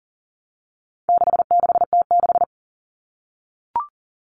Morse code, heard as a keyed tone, about 40 wpm, 700 Hz. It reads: 66T6
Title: Common Cut Numbers - 4-Digit 40wpm